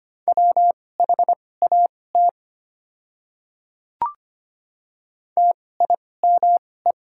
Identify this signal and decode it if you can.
Morse code, heard as a keyed tone, about 25 wpm, spaced as for 18 wpm, 700 Hz. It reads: WHAT TIME
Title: 2x Character Spacing - Top 100 Words - 25wpm